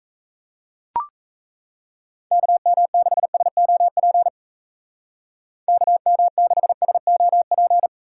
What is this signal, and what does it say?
Morse code, heard as a keyed tone, about 37 wpm, 700 Hz. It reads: KM6SOP KM6SOP